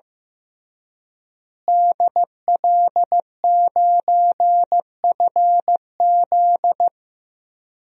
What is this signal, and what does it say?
Morse code, heard as a keyed tone, about 15 wpm, 700 Hz. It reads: DL9FZ
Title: DXCC-100 Call Signs - 5 Characters - Repeated 3x - 15wpm